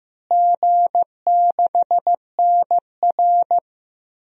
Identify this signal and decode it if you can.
Morse code, heard as a keyed tone, about 15 wpm, 700 Hz. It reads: G6NR
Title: DXCC-100 Call Signs - 4 Characters - Repeated 3x - 15wpm